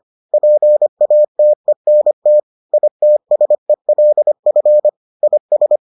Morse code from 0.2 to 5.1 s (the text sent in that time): PATENT ITSELF